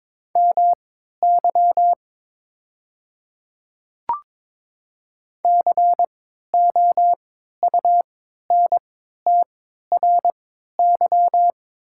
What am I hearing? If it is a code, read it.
Morse code, heard as a keyed tone, about 22 wpm, spaced as for 13 wpm, 690 Hz. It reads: MY COUNTRY
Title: 3x Character Spacing - Top 500 Words - 22wpm